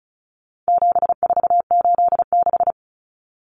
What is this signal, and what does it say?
Morse code, heard as a keyed tone, about 34 wpm, 700 Hz. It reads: 7486
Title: Common Cut Numbers - 4-Digit 35wpm